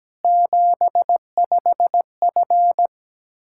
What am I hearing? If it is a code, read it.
Morse code, heard as a keyed tone, about 17 wpm, 700 Hz. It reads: 75F
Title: K1USN SST QSO Practice 17wpm